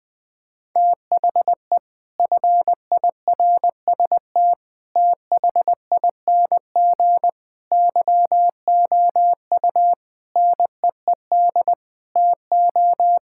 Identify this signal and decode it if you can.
Morse code, heard as a keyed tone, about 21 wpm, 700 Hz. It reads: THE FIRST THING YOU NEED TO
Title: Sets of 8 Words - 20wpm